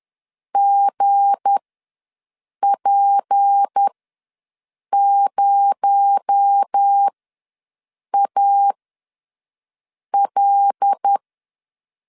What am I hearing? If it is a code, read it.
Morse code, heard as a keyed tone, about 10 wpm, spaced as for 6 wpm, 800 Hz. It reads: GP0AL
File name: LEZIONI / 7 GF9Z / 7 GF9 ws 10wpm 4034